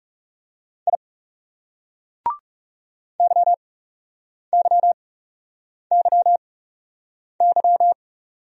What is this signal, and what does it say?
Morse code, heard as a keyed tone, about 40 wpm, spaced as for 8 wpm, 700 Hz. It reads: I YYYY